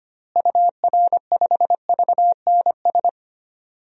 Morse code, heard as a keyed tone, about 25 wpm, 700 Hz. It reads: UR5VNS
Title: DXCC-100 Call Signs - 6 Characters - Repeated 2x - 25wpm